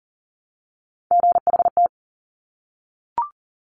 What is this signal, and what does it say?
Morse code, heard as a keyed tone, about 38 wpm, 700 Hz. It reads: GHT
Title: Top Three Letter Combinations in Words 40wpm